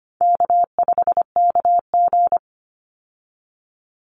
Morse code, heard as a keyed tone, about 25 wpm, 700 Hz. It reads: K5KG
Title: Easy Call Signs - 25wpm